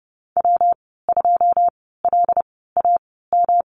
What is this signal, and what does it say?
Morse code, heard as a keyed tone, about 30 wpm, spaced as for 17 wpm, 700 Hz. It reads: W2LAM
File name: Lesson 19g - Letter w - Call Signs 30wpm